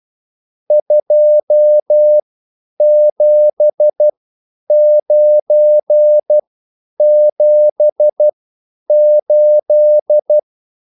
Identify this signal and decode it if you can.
Morse code, heard as a keyed tone, about 12 wpm, spaced as for 9 wpm, 600 Hz. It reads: 27978